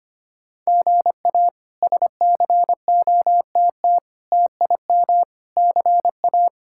Morse code, heard as a keyed tone, about 25 wpm, 700 Hz. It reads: GA SCOTT TIM CA
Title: K1USN SST QSO Practice - 25wpm